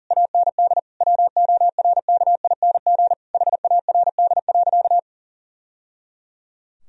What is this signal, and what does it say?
Morse code, heard as a keyed tone, about 40 wpm, 700 Hz. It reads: AND WORKING HARD.
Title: Sentences from Top 1000 Words - Review Part 4 - 40wpm